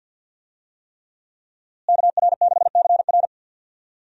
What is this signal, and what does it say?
Morse code, heard as a keyed tone, about 51 wpm, 700 Hz. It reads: KR6CR
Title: USA Call Signs - 5 Characters - Repeated 3x - 50wpm